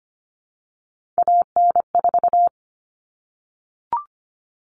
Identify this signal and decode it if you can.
Morse code, heard as a keyed tone, about 25 wpm, 700 Hz. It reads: AN4
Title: All Cut Numbers - 3-Digit 25wpm